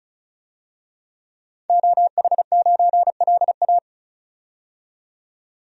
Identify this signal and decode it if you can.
Morse code, heard as a keyed tone, about 34 wpm, 700 Hz. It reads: OH9LA
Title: DXCC-100 Call Signs - 5 Characters - 35wpm